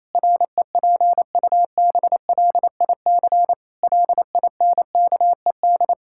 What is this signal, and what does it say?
Morse code, heard as a keyed tone, about 28 wpm, 700 Hz. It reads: REPUBLIC LINKED
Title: Pangram Challenge v2 - 28wpm